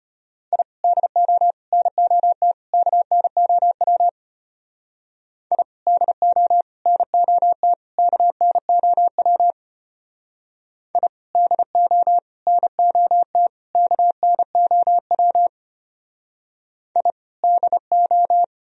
Morse code, heard as a keyed tone, about 30 wpm, 700 Hz. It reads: I DO NOT KNOW I DO NOT KNOW I DO NOT KNOW I DO
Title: Speed-Racing - Sets of 4 Words 25wpm